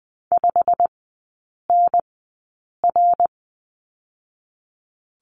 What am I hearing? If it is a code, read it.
Morse code, heard as a keyed tone, about 20 wpm, spaced as for 8 wpm, 700 Hz. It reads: 5NR
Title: 087 Lesson 15e - Number 5 - Three Characters 20wpm